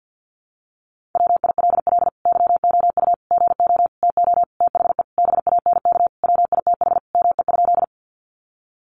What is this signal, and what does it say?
Morse code, heard as a keyed tone, about 50 wpm, 700 Hz. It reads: WILL YOU GO TO THE BANK WITH ME?
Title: Sentences from Top 900 Words 50wpm